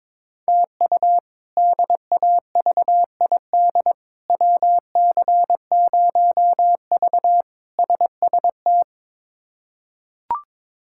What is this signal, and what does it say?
Morse code, heard as a keyed tone, about 22 wpm, 700 Hz. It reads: TU DAVID WC0V SST
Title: K1USN SST QSO Practice - 22wpm